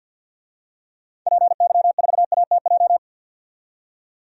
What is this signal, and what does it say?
Morse code, heard as a keyed tone, about 52 wpm, 700 Hz. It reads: PY4ATJ
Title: DXCC-100 Call Signs - 6 Characters - 50wpm